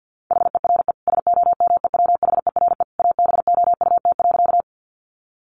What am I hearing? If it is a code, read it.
Morse code, heard as a keyed tone, about 50 wpm, 700 Hz. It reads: HERE SOMEWHERE ABOUT.
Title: Sentences from Top 1400 Words 50wpm